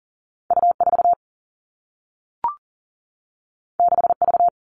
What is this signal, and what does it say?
Morse code, heard as a keyed tone, about 40 wpm, 700 Hz. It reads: U4 6V
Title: Two Challenging Letter-Number Combinations 40wpm